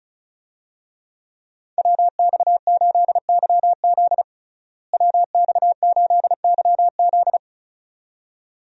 Morse code, heard as a keyed tone, about 34 wpm, 700 Hz. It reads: WX8YZ WX8YZ
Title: USA Call Signs - 5 Characters - Repeated 2x - 35wpm